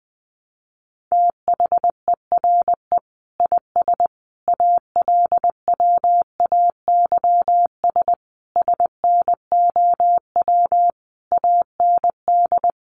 Morse code, heard as a keyed tone, about 20 wpm, 700 Hz. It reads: THERE IS ALWAYS SNOW AND
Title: Sentences from Top 1000 Words 20wpm